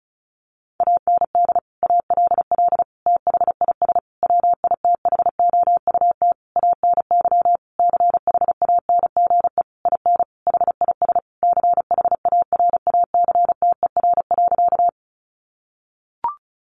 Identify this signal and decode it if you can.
Morse code, heard as a keyed tone, about 34 wpm, 700 Hz. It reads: AND ALL THIS WITHOUT ANY CHANGE IN HIS CHARACTER.
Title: Sentences from Top 1200 Words 35wpm